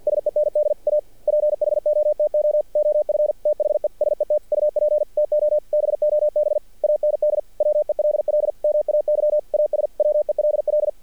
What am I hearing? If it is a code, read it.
Morse code, heard as a keyed tone, about 50 wpm, 600 Hz. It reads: SEND A PHOTO OF THE SET UP TO BOB AND WELL MAY AS WELL
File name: archived / CRSnet